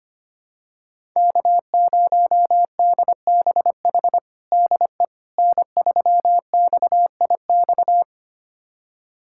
Morse code, heard as a keyed tone, about 25 wpm, 700 Hz. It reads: K0DBH DE N3XIX
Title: USA Callsign DE Callsign - 5 Characters - 25wpm